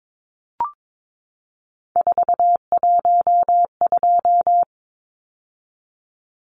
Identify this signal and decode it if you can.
Morse code, heard as a keyed tone, about 22 wpm, 700 Hz. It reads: 412
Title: Numbers - 3-Digit 22wpm